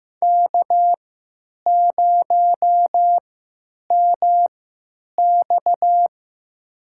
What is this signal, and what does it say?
Morse code, heard as a keyed tone, about 15 wpm, spaced as for 9 wpm, 700 Hz. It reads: K0MX
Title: USA Call Signs - 3x Character Spacing - 15wpm